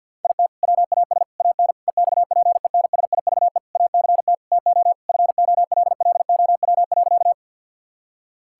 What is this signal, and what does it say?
Morse code, heard as a keyed tone, about 50 wpm, 700 Hz. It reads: IT WAS AN EXPENSIVE ACT TO FOLLOW.
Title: Sentences from Top 1000 Words - Review Part 6 - 50wpm